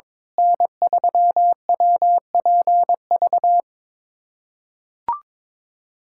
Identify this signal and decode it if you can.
Morse code, heard as a keyed tone, about 22 wpm, 700 Hz. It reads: N3WPV
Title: USA Callsign DE Callsign - 5 Characters - 22wpm